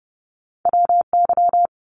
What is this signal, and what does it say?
Morse code, heard as a keyed tone, about 32 wpm, 700 Hz. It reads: WY
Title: ARRL Sections 30wpm